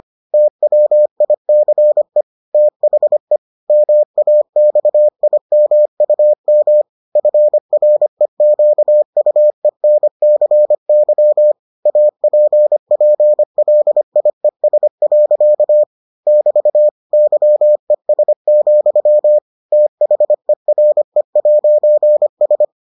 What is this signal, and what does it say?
Morse code, heard as a keyed tone, about 25 wpm, 600 Hz. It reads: TWICE THE MAXIMUM FREQUENCY APPLIES. <BT> YES, THERE'S